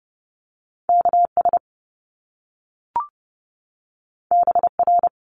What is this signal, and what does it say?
Morse code, heard as a keyed tone, about 30 wpm, 700 Hz. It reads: KS BR
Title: Top Two Letter Combinations in Words 30wpm